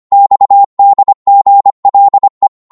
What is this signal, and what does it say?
Morse code, heard as a keyed tone, about 25 wpm, 800 Hz. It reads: XDGLE
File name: Morse / ltr-25wpm